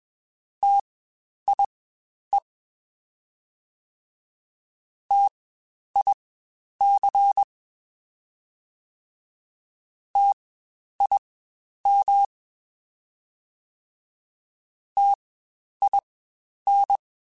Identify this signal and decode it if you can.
Morse code, heard as a keyed tone, about 21 wpm, spaced as for 10 wpm, 800 Hz. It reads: TIE TIC TIM TIN
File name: Spaced Alphas / Spaced Alphas mp3s / Ts sp @20